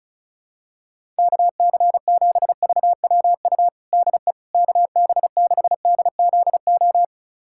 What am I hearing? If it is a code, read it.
Morse code, heard as a keyed tone, about 36 wpm, 700 Hz. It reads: KC7VWU DE KB6DZO
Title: USA Callsign DE Callsign 35wpm